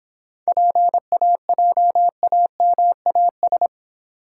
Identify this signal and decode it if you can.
Morse code, heard as a keyed tone, about 26 wpm, 700 Hz. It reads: PAJAMAS